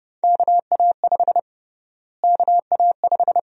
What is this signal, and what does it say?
Morse code, heard as a keyed tone, about 30 wpm, 700 Hz. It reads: KA5 KA5
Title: USA Call Sign Prefixes - Repeated 2x - 30wpm